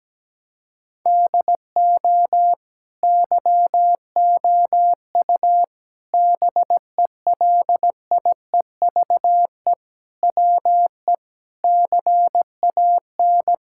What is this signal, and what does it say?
Morse code, heard as a keyed tone, about 17 wpm, 700 Hz. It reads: DO YOU BELIEVE WE CAN
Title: Sentences from Top 1000 Words - Review Part 6 - 17wpm